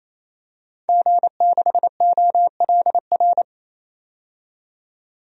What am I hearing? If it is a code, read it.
Morse code, heard as a keyed tone, about 28 wpm, 700 Hz. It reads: G6OLR
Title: DXCC-100 Call Signs - 5 Characters - 28wpm